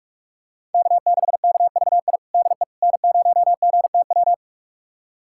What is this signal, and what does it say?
Morse code, heard as a keyed tone, about 46 wpm, 700 Hz. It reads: K6KVI DE N0GTW
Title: USA Callsign DE Callsign - 5 Characters - 45wpm